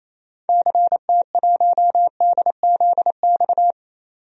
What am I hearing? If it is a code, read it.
Morse code, heard as a keyed tone, about 28 wpm, 700 Hz. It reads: CT1DZX
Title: DXCC-100 Call Signs - 6 Characters - 28wpm